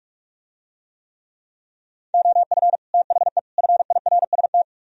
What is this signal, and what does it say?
Morse code, heard as a keyed tone, about 46 wpm, 700 Hz. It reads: OF THE FIRST